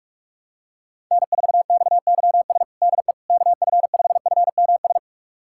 Morse code, heard as a keyed tone, about 46 wpm, 700 Hz. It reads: N4XYS DE KF5FMS
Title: USA Callsign DE Callsign 45wpm